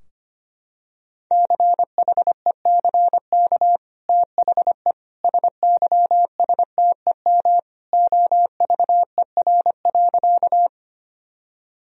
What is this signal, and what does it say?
Morse code, heard as a keyed tone, about 25 wpm, 700 Hz. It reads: CHECK THE SYSTEM OVER.
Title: Sentences from Top 1000 Words - Review Part 3 - 25wpm